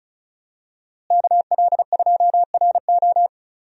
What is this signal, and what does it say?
Morse code, heard as a keyed tone, about 36 wpm, 700 Hz. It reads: KL2RO
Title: USA Call Signs - 5 Characters - Repeated 3x - 35wpm